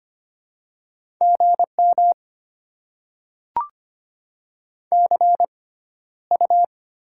Morse code, heard as a keyed tone, about 25 wpm, spaced as for 13 wpm, 700 Hz. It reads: GM CU
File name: Top 20 QSO Elements - 6x Character Spacing - 25wpm